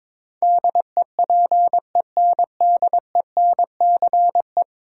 Morse code, heard as a keyed tone, about 22 wpm, 700 Hz. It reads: DEPENDENCE
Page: morse-code-ninja-practice-22wpm